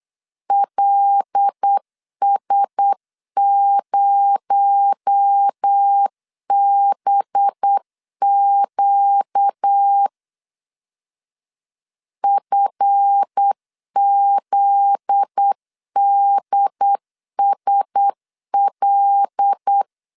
Morse code, heard as a keyed tone, about 8 wpm, 800 Hz. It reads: LS0BQ FZDSL
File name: LEZIONI / 7 GF9Z / 7 GF9Z 8wpm 3213